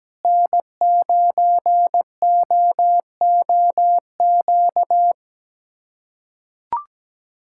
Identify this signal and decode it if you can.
Morse code, heard as a keyed tone, about 17 wpm, 700 Hz. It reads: N9OOQ
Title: USA Callsign DE Callsign 17wpm